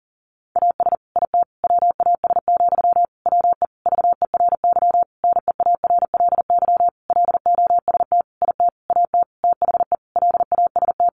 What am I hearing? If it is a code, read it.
Morse code, heard as a keyed tone, about 40 wpm, 700 Hz. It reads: AS IT WAS, WE VERY NEARLY LOST IT AT THE LAST
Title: Sentences from Top 1400 Words 40wpm